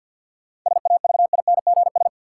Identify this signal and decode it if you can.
Morse code, heard as a keyed tone, about 53 wpm, 700 Hz. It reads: SAVINGS
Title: Sentences from Top 1200 Words - New Word Drill Alt - 50wpm